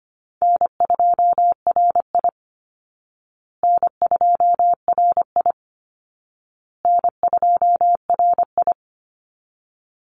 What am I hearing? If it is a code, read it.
Morse code, heard as a keyed tone, about 25 wpm, 700 Hz. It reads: N2RI N2RI N2RI
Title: Easy Call Signs Repeated 3x - 25wpm